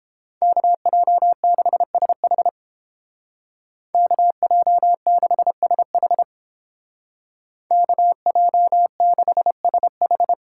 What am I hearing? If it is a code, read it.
Morse code, heard as a keyed tone, about 29 wpm, 700 Hz. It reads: KJ6SH KJ6SH KJ6SH